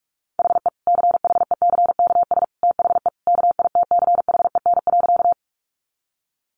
Morse code, heard as a keyed tone, about 44 wpm, 700 Hz. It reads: HE CHECKS THE KITCHEN.
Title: Easy Phrases 45wpm